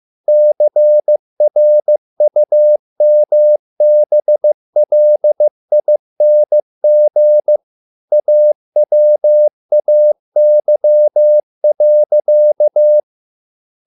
Morse code, heard as a keyed tone, about 15 wpm, 600 Hz. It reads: CRUMBLING AWAY.